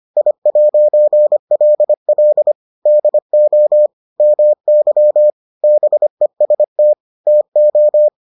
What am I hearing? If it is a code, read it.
Morse code, heard as a keyed tone, about 25 wpm, 600 Hz. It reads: I'LL DO MY BEST TO